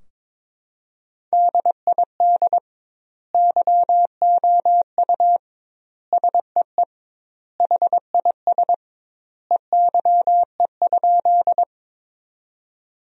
Morse code, heard as a keyed tone, about 22 wpm, 700 Hz. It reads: DID YOU SEE HIS EYE?
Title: 2x Word-Spacing - Sentences from Top 300 Words - 22wpm